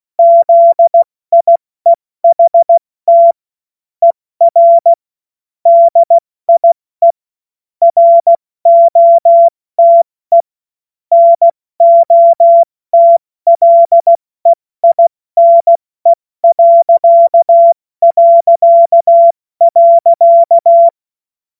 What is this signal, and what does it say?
Morse code, heard as a keyed tone, about 16 wpm, 680 Hz. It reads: ZIEHT ER DIE ROTE NOTLEINE...
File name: Morsekurs / Witze / Witz 02. Dezember 2025 70 F80